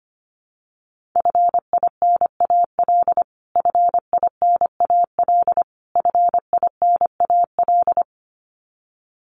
Morse code, heard as a keyed tone, about 25 wpm, 700 Hz. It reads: FINAL FINAL FINAL
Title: Top 500 Words Repeated 25wpm